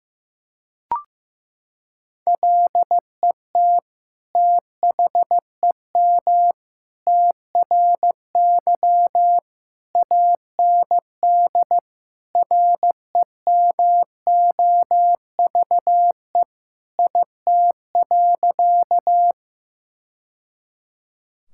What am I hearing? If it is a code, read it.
Morse code, heard as a keyed tone, about 15 wpm, 700 Hz. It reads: LET THEM TRY AND REMOVE IT.
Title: Sentences from Top 1000 Words - Review Part 5 - 15wpm